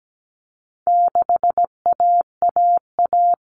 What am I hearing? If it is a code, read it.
Morse code, heard as a keyed tone, about 17 wpm, 700 Hz. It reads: 6AAA
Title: Common Cut Numbers - 4-Digit 17wpm